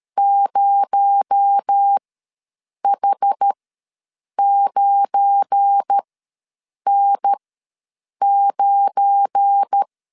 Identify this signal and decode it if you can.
Morse code, heard as a keyed tone, about 13 wpm, spaced as for 7 wpm, 800 Hz. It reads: 0H9N9
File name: LEZIONI / 7 GF9Z / 7 GF9 ws 12wpm 3344